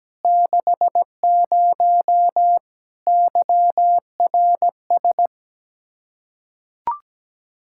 Morse code, heard as a keyed tone, about 17 wpm, 700 Hz. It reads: 60 YRS
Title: K1USN SST QSO Practice 17wpm